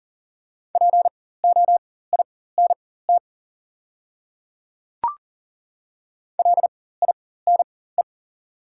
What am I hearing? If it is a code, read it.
Morse code, heard as a keyed tone, about 40 wpm, spaced as for 19 wpm, 690 Hz. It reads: POINT LINE
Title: 4x Character Spacing - Top 200 Words - 40wpm